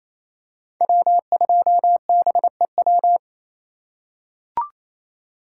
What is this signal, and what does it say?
Morse code, heard as a keyed tone, about 28 wpm, 700 Hz. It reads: W2BEW
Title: USA Call Signs - 5 Characters - 2x Character Spacing - 28wpm